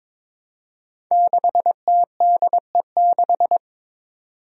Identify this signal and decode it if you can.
Morse code, heard as a keyed tone, about 22 wpm, 700 Hz. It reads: 6TDE6
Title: Five Challenging Letter-Number Combinations 22wpm